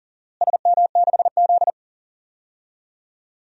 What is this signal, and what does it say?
Morse code, heard as a keyed tone, about 39 wpm, 700 Hz. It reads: SM6Z
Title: DXCC-100 Call Signs - 4 Characters - Repeated 2x - 40wpm